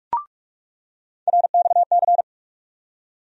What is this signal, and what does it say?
Morse code, heard as a keyed tone, about 46 wpm, 700 Hz. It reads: RXC
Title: USA Call Sign Postfixes - 3 Letters 45wpm